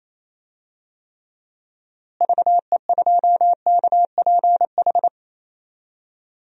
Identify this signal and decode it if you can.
Morse code, heard as a keyed tone, about 28 wpm, 700 Hz. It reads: VE2KPH
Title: DXCC-100 Call Signs - 6 Characters - Repeated 2x - 28wpm